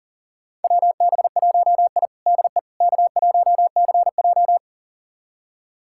Morse code, heard as a keyed tone, about 40 wpm, 700 Hz. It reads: WB1I DE K1CJ
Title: USA Callsign DE Callsign - 4 Characters - 40wpm